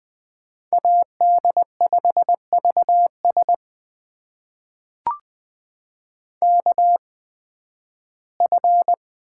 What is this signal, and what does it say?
Morse code, heard as a keyed tone, about 20 wpm, spaced as for 13 wpm, 700 Hz. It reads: AD5VS KF
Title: USA Call Signs - 5 Characters - 8x Character Spacing - 20wpm